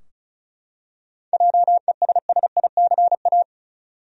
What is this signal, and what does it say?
Morse code, heard as a keyed tone, about 34 wpm, 700 Hz. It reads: JESSICA
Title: USA Common Names - 35wpm